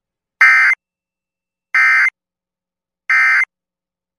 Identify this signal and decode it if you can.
Emergency Alert System (SAME) tones, heard as harsh buzzing bursts. It is heard from 0.4 to 3.4 s.